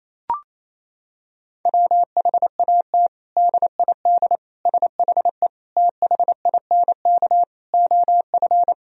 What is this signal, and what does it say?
Morse code, heard as a keyed tone, about 27 wpm, 700 Hz. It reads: WHAT DID SHE THINK OF
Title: Sentences from Top 1000 Words - Review Part 4 - 28wpm